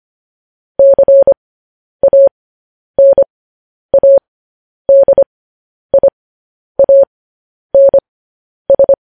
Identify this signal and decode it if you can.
Morse code, heard as a keyed tone, about 25 wpm, spaced as for 10 wpm, 550 Hz. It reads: CANADIANS
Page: news headlines in morse code at 10 wpm